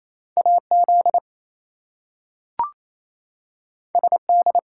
Morse code, heard as a keyed tone, about 28 wpm, 700 Hz. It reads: AZ SD
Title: US States and Canadian Provinces 28wpm